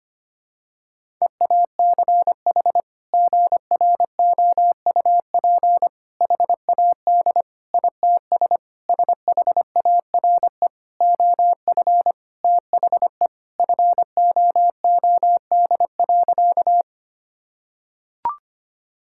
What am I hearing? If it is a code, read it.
Morse code, heard as a keyed tone, about 25 wpm, 700 Hz. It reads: EACH GROUP HAD ITS SHARE OF THE FOOD.
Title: Sentences from Top 1000 Words - Review Part 6 - 25wpm